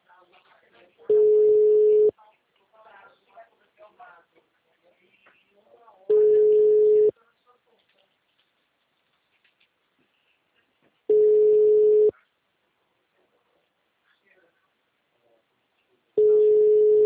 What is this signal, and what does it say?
Ringback tone (Europe, 425 Hz), with speech.
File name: out- -2008- - - .15693